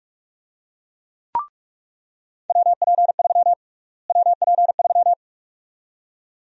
Morse code, heard as a keyed tone, about 46 wpm, 700 Hz. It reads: WP3 WP3
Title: USA Call Sign Prefixes - Repeated 2x - 45wpm